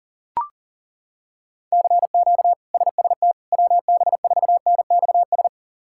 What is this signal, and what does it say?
Morse code, heard as a keyed tone, about 37 wpm, 700 Hz. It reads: CQ SST WB4NXS
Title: K1USN SST QSO Practice - 40wpm